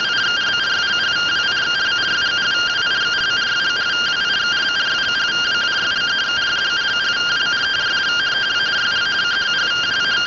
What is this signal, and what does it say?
Two-tone teleprinter signal, about 170 Hz shift.